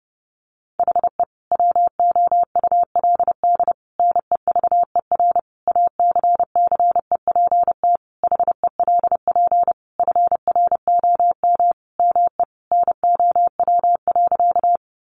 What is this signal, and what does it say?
Morse code, heard as a keyed tone, about 28 wpm, 700 Hz. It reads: HE WOULD NEVER ACCEPT HELP FROM ME NOW.
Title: Sentences from Top 1300 Words 30wpm